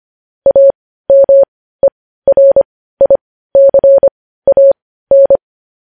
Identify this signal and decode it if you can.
Morse code, heard as a keyed tone, about 25 wpm, spaced as for 15 wpm, 550 Hz. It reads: AMERICAN
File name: podcast / News headlines-1x-25x15